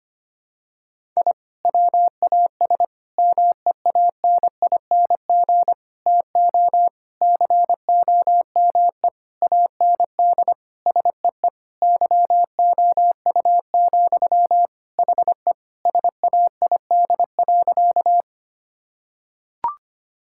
Morse code, heard as a keyed tone, about 25 wpm, 700 Hz. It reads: I WAS MEANING TO COME AND SEE YOU, HE SAID.
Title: Sentences from Top 1000 Words - Review Part 6 - 25wpm